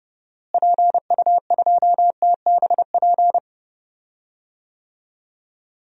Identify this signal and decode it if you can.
Morse code, heard as a keyed tone, about 32 wpm, 700 Hz. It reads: PU2TBP
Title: DXCC-100 Call Signs - 6 Characters - Repeated 2x - 30wpm